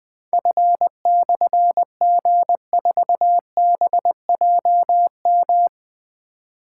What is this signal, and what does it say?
Morse code, heard as a keyed tone, about 20 wpm, 700 Hz. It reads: F/G4BJM
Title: Difficult Call Signs 20wpm